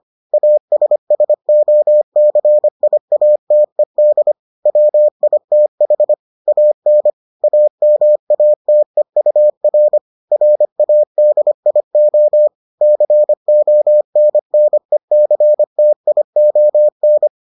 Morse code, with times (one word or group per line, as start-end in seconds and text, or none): 0.3-4.3 ASSOCIATED
4.7-6.1 WITH
6.5-7.1 AN
7.4-10.0 AMATEUR
10.3-12.5 RADIO
12.8-17.3 CONNECTION